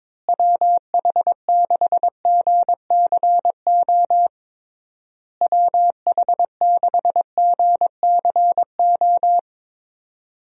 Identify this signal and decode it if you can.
Morse code, heard as a keyed tone, about 22 wpm, 700 Hz. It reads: WH6GCO WH6GCO
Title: USA Call Signs - 6 Characters - Repeated 3x - 22wpm